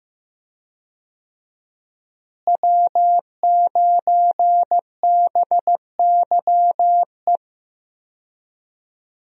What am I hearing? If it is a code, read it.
Morse code, heard as a keyed tone, about 15 wpm, 700 Hz. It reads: W9BYE
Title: USA Call Signs - 5 Characters - Repeated 3x - 15wpm